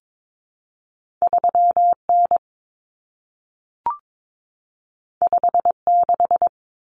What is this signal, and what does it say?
Morse code, heard as a keyed tone, about 22 wpm, 700 Hz. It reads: 3N 56
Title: Common Cut Numbers - 2-Digit 22wpm